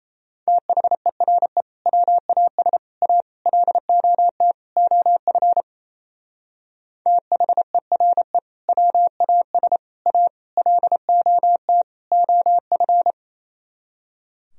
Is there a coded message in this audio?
Yes: Morse code from 0.5 to 13.1 s, heard as a keyed tone, about 37 wpm, 700 Hz. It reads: THERE WAS A LOT OF THERE WAS A LOT OF